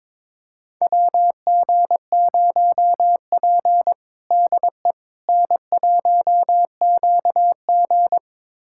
Morse code, heard as a keyed tone, about 21 wpm, 700 Hz. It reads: WG0P DE N1QG
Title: USA Callsign DE Callsign - 4 Characters - 22wpm